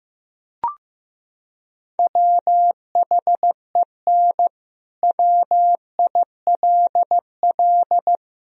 Morse code, heard as a keyed tone, about 15 wpm, 700 Hz. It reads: WHEN WILL
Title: Sentences from Top 1000 Words - Review Part 4 - 15wpm